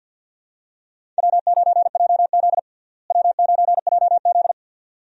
Morse code, heard as a keyed tone, about 53 wpm, 700 Hz. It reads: W9JZ W9JZ
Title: USA Call Signs - 4 Characters - Repeated 2x - 50wpm